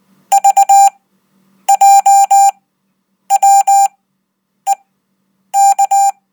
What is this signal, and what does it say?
Morse code, heard as a keyed tone, about 19 wpm, spaced as for 8 wpm, 790 Hz. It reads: VJWEK
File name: Code